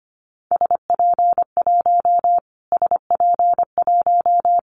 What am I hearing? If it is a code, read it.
Morse code, heard as a keyed tone, about 25 wpm, 700 Hz. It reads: SP1 SP1
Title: Callsign Prefixes for Top 25 DXCC - Repeated 3x - 25wpm